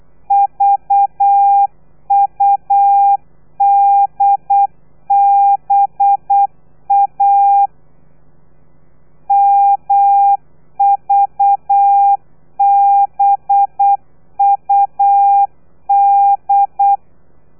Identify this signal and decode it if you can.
Morse code, heard as a keyed tone, about 8 wpm, 800 Hz. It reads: VUDBA MVBUD
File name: telegrafia / i2tfj / Lez4 35